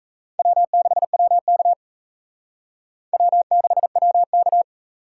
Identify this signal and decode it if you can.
Morse code, heard as a keyed tone, about 35 wpm, 700 Hz. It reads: W6WK W6WK
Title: Speed Racing - USA 1x2 Call Signs 28wpm